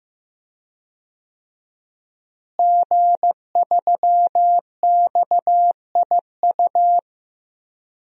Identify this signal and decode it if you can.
Morse code, heard as a keyed tone, about 15 wpm, 700 Hz. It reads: G3XIU
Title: DXCC-100 Call Signs - 5 Characters - Repeated 2x - 15wpm